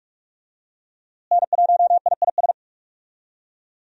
Morse code, heard as a keyed tone, about 46 wpm, 700 Hz. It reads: N1IIS
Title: USA Call Signs - 5 Characters - Repeated 3x - 45wpm